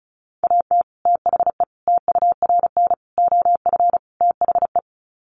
Morse code, heard as a keyed tone, about 34 wpm, 700 Hz. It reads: AT THE TURN OF THE
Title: Sets of 5 Words 35wpm